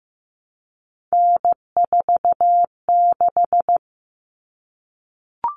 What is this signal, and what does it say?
Morse code, heard as a keyed tone, about 15 wpm, 700 Hz. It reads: N46
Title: Common Cut Numbers - 3-Digit 15wpm